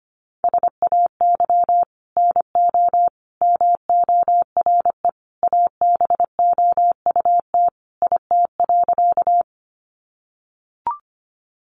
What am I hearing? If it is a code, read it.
Morse code, heard as a keyed tone, about 25 wpm, 700 Hz. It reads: SAY NO MORE ABOUT IT.
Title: Sentences from Top 200 Words 25wpm